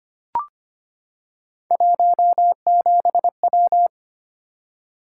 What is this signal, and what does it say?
Morse code, heard as a keyed tone, about 25 wpm, 700 Hz. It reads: 17W
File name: Three Letter-Number Combinations - 25wpm